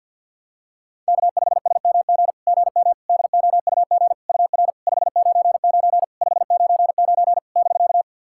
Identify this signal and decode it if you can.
Morse code, heard as a keyed tone, about 50 wpm, 690 Hz. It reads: K5SMG GM DOUG UR 599 599 <BK>
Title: SOTA QSO Practice 50wpm